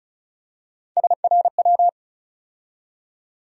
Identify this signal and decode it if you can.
Morse code, heard as a keyed tone, about 36 wpm, 700 Hz. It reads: SRW
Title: USA Call Sign Postfixes - 3 Letters 35wpm